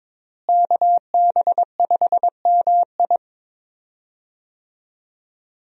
Morse code, heard as a keyed tone, about 22 wpm, 700 Hz. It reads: KB5MI
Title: USA Call Signs - 5 Characters - Repeated 3x - 22wpm